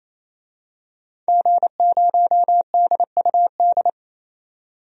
Morse code, heard as a keyed tone, about 28 wpm, 700 Hz. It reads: G0DUD
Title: DXCC-100 Call Signs - 5 Characters - 28wpm